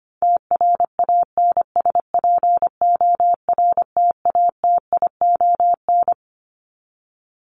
Morse code, heard as a keyed tone, about 25 wpm, 700 Hz. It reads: TRANSPORTATION
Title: Top 1501-1750 Words 25wpm